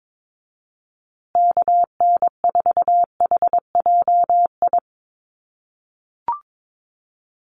Morse code, heard as a keyed tone, about 22 wpm, 700 Hz. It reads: KN4HJI
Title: USA Call Signs Repeated 2x 22wpm